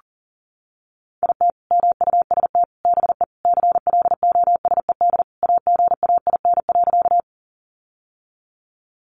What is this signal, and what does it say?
Morse code, heard as a keyed tone, about 40 wpm, 700 Hz. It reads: IT MUST BE CLOSED AGAIN.
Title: Sentences from Top 1400 Words 40wpm